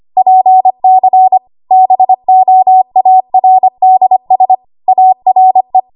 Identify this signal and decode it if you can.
Morse code, heard as a keyed tone, about 25 wpm, 750 Hz. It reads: PC BOARDS ARE